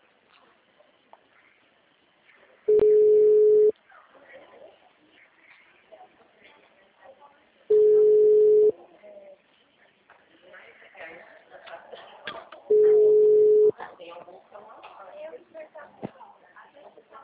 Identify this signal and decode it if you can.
Ringback tone (Europe, 425 Hz), with speech.
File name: out- -2014- - - .9248